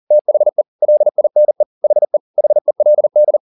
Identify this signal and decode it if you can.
Morse code, heard as a keyed tone, about 40 wpm, 600 Hz. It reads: THE LINE HE HELD